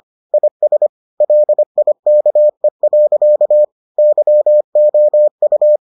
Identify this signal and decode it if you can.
Morse code, heard as a keyed tone, about 25 wpm, 600 Hz. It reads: IS LIKE. YOU